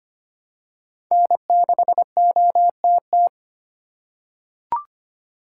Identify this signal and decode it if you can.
Morse code, heard as a keyed tone, about 25 wpm, 700 Hz. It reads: N6OTT
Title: USA Call Signs - 4x Character Spacing - 25wpm